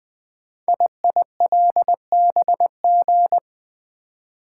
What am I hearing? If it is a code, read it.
Morse code, heard as a keyed tone, about 20 wpm, 700 Hz. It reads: IILBG
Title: Code Groups - 5 Letters - 20wpm